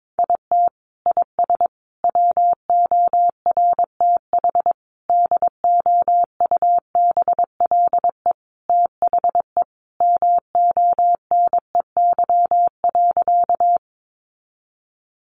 Morse code, heard as a keyed tone, about 22 wpm, 700 Hz. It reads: IT IS WORTH DOUBLE THE MONEY.
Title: Sentences from Top 1400 Words 22wpm